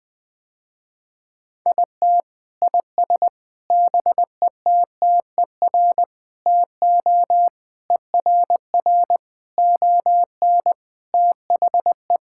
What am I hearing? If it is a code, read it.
Morse code, heard as a keyed tone, about 21 wpm, 700 Hz. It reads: IT IS BETTER TO ERR ON THE
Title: Sentences with 7-Word Phrases - Spoken Beginning - 20wpm